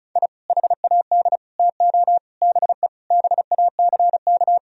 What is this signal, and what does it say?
Morse code, heard as a keyed tone, about 36 wpm, 700 Hz. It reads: I HAD TO BE BACK
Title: Sentences from Top 1000 Words - Review Part 4 - 35wpm